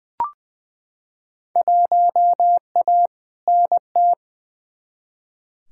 Morse code, heard as a keyed tone, about 20 wpm, 700 Hz. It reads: 1A NT
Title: Field Day Exchanges Encore - 20wpm